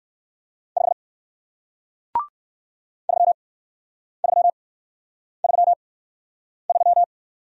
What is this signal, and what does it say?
Morse code, heard as a keyed tone, about 56 wpm, spaced as for 9 wpm, 700 Hz. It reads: 5 3333
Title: Speed-Racing Single Letter-Number 45wpm